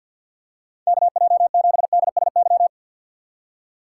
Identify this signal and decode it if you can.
Morse code, heard as a keyed tone, about 53 wpm, 700 Hz. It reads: KJ7DSY
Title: USA Call Signs - 2x Character Spacing - 50wpm